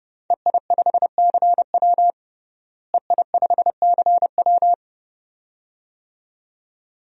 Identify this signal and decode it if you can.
Morse code, heard as a keyed tone, about 30 wpm, 700 Hz. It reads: EI5CW EI5CW
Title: DXCC-100 Call Signs - 5 Characters - Repeated 2x - 30wpm